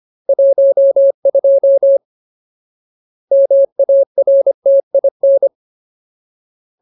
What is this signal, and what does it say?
Morse code, heard as a keyed tone, about 25 wpm, 550 Hz. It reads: 12 MARTIN
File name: Ultimate Word List / 25WPM Ultimate